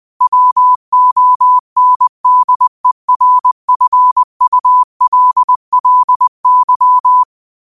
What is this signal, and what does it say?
Morse code, heard as a keyed tone, about 20 wpm, 1000 Hz. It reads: WONDERFULLY
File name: qotdfiles / 20wpm